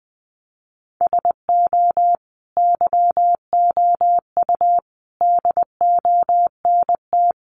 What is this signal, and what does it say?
Morse code, heard as a keyed tone, about 20 wpm, 700 Hz. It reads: SO YOU DONT
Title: Sentences from Top 1000 Words 20wpm